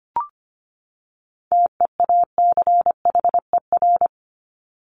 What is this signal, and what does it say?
Morse code, heard as a keyed tone, about 25 wpm, 700 Hz. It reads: TEACHER
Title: Top 751-1000 Words 25wpm